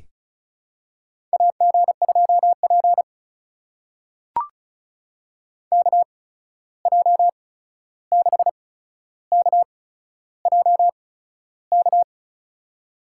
Morse code, heard as a keyed tone, about 34 wpm, spaced as for 13 wpm, 700 Hz. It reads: AG2P KJ6KJK
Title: USA Call Signs - 8x Character Spacing - 35wpm